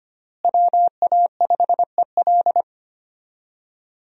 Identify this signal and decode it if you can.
Morse code, heard as a keyed tone, about 25 wpm, 700 Hz. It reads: WA5EL
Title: USA Call Signs - 5 Characters - Repeated 3x - 25wpm